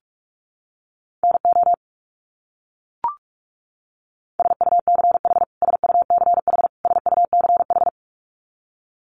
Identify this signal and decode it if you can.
Morse code, heard as a keyed tone, about 44 wpm, 700 Hz. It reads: NO SUCH SUCH SUCH
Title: Top 200 Words Repeated 45wpm